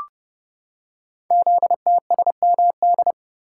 Morse code, heard as a keyed tone, about 30 wpm, 700 Hz. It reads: ZTSMD
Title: Code Groups - 5 Letters - 30wpm